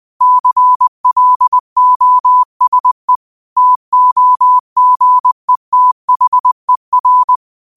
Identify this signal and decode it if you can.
Morse code, heard as a keyed tone, about 20 wpm, 1000 Hz. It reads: CLOSE TOGETHER